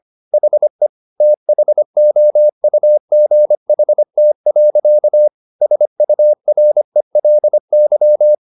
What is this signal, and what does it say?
Morse code, heard as a keyed tone, about 25 wpm, 600 Hz. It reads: HE THOUGHT. SURELY
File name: the old man and the sea part 1 25wpm